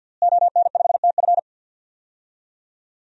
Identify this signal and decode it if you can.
Morse code, heard as a keyed tone, about 47 wpm, 700 Hz. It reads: ON5TF
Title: DXCC-100 Call Signs - 5 Characters - Repeated 2x - 50wpm